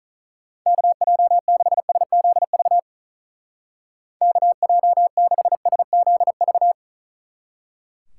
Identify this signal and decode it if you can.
Morse code, heard as a keyed tone, about 45 wpm, 700 Hz. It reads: KJ6SZV KJ6SZV